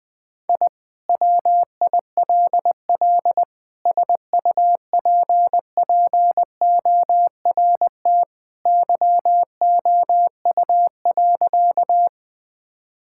Morse code, heard as a keyed tone, about 20 wpm, 700 Hz. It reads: I WILL SUPPORT YOU.
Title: Sentences from Top 1000 Words - Review Part 5 - 20wpm